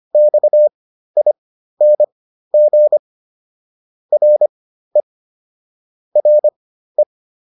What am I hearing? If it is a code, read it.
Morse code, heard as a keyed tone, about 25 wpm, spaced as for 13 wpm, 600 Hz. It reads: XING RE RE